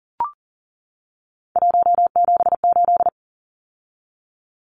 Morse code, heard as a keyed tone, about 40 wpm, 700 Hz. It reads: 178
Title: Numbers - 3-Digit 40wpm